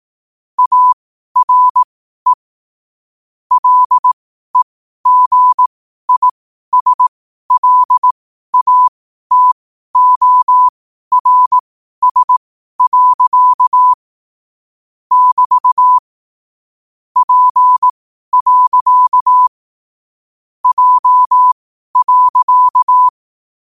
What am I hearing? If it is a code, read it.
Morse code, heard as a keyed tone, about 18 wpm, spaced as for 12 wpm, 1000 Hz. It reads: ARE LEGISLATORS. <BT> P. J.
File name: qotdfiles / 12wpm